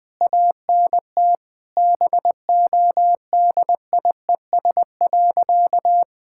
Morse code, heard as a keyed tone, about 21 wpm, 700 Hz. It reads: ANT BODIES.
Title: All-Age Appropriate Jokes - 20wpm